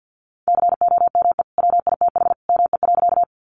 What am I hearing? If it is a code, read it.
Morse code, heard as a keyed tone, about 50 wpm, 700 Hz. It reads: COME WITH ME.
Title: Sentences from Top 200 Words 50wpm